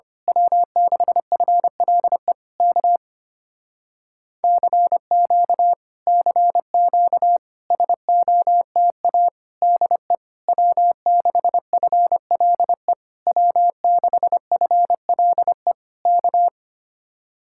Morse code, heard as a keyed tone, about 24 wpm, 700 Hz. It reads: W6FLE K CQ CQ SOTA DE W6FLE W6FLE K
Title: Speed-Racing Common QSO Phrases - 25wpm